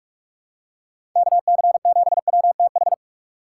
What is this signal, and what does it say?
Morse code, heard as a keyed tone, about 44 wpm, 700 Hz. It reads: KC7WTH
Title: USA Call Signs - 6 Characters - 6x Character Spacing - 45wpm